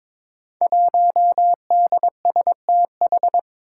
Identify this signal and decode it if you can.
Morse code, heard as a keyed tone, about 22 wpm, 700 Hz. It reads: 1DSTH
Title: Five Challenging Letter-Number Combinations 22wpm